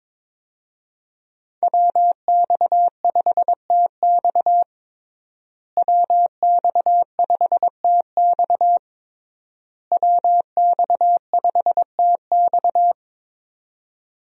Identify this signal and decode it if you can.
Morse code, heard as a keyed tone, about 22 wpm, 700 Hz. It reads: WX5TX WX5TX WX5TX
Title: USA Call Signs - 5 Characters - Repeated 3x - 22wpm